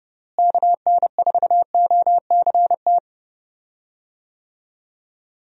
Morse code, heard as a keyed tone, about 30 wpm, 700 Hz. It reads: KN4OCT
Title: USA Call Signs - 6 Characters - Repeated 3x - 30wpm